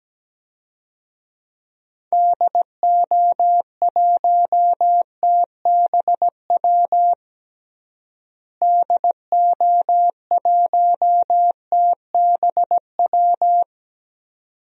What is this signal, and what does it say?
Morse code, heard as a keyed tone, about 17 wpm, 700 Hz. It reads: DO1TBW DO1TBW
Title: DXCC-100 Call Signs - 6 Characters - Repeated 2x - 17wpm